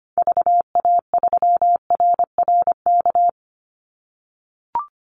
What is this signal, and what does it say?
Morse code, heard as a keyed tone, about 25 wpm, 700 Hz. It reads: VA3RRK
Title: Call Signs - 25wpm